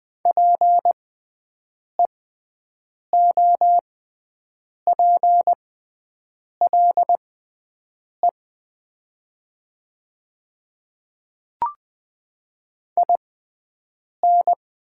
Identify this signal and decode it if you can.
Morse code, heard as a keyed tone, about 20 wpm, spaced as for 7 wpm, 700 Hz. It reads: PEOPLE IN
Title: 6x Character Spacing - Top 100 Words - 20wpm